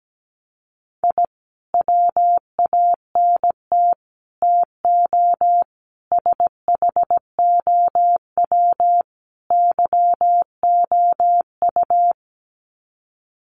Morse code, heard as a keyed tone, about 17 wpm, 700 Hz. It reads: I WANT TO SHOW YOU
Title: Sets of 5 Words 17wpm